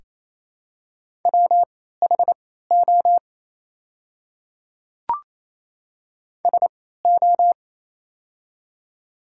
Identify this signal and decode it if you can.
Morse code, heard as a keyed tone, about 28 wpm, spaced as for 16 wpm, 700 Hz. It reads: WHO SO
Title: 3x Character Spacing - Top 100 Words - 28wpm